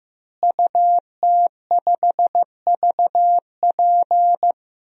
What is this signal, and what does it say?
Morse code, heard as a keyed tone, about 15 wpm, 700 Hz. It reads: UT5VP
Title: DXCC-100 Call Signs - 5 Characters - Repeated 3x - 15wpm